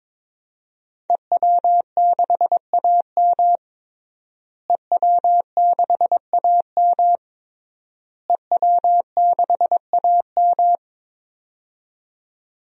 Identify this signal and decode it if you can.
Morse code, heard as a keyed tone, about 22 wpm, 700 Hz. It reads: EW6AM EW6AM EW6AM
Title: DXCC-100 Call Signs - 5 Characters - Repeated 3x - 22wpm